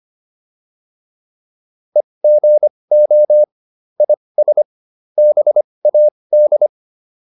Morse code, heard as a keyed tone, about 25 wpm, spaced as for 20 wpm, 600 Hz. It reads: EGO IS BAD